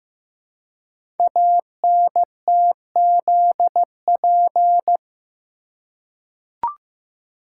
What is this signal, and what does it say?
Morse code, heard as a keyed tone, about 15 wpm, 700 Hz. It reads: ANTZP
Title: Code Groups - 5 Letters - 15wpm